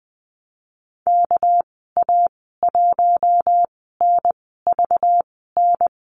Morse code, heard as a keyed tone, about 19 wpm, spaced as for 14 wpm, 700 Hz. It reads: KA1NVN
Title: 207 Lesson 33g - Letter j - Call Signs 20wpm